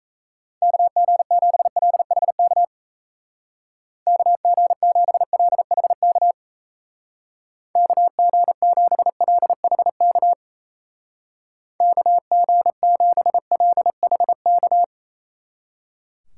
Morse code, heard as a keyed tone, about 35 wpm, 700 Hz. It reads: KG7LHK KG7LHK KG7LHK KG7LHK
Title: Speed Racing - USA Call Signs 28wpm